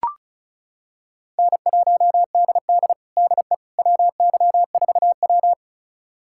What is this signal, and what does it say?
Morse code, heard as a keyed tone, about 36 wpm, 700 Hz. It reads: N1DD DE WY4W
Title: USA Callsign DE Callsign - 4 Characters - 35wpm